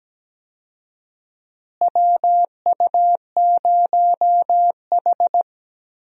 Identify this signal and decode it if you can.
Morse code, heard as a keyed tone, about 17 wpm, 700 Hz. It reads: WU0H
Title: USA Call Signs - 4 Characters - Repeated 2x - 17wpm